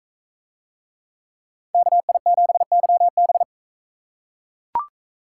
Morse code, heard as a keyed tone, about 40 wpm, 700 Hz. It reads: KI7YB